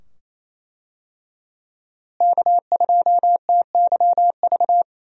Morse code, heard as a keyed tone, about 27 wpm, 700 Hz. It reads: K2TYV
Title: USA Call Signs - 5 Characters - Repeated 2x - 28wpm